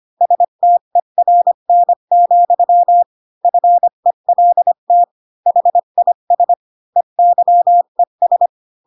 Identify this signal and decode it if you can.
Morse code, heard as a keyed tone, about 25 wpm, 700 Hz. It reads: STERN, FELT HIS EYES
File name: Pinocchio / Pinocchio ch8 20 25 ik1hge